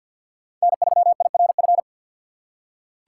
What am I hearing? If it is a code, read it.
Morse code, heard as a keyed tone, about 47 wpm, 700 Hz. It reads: N3IRF
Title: USA Call Signs - 5 Characters - 2x Character Spacing - 50wpm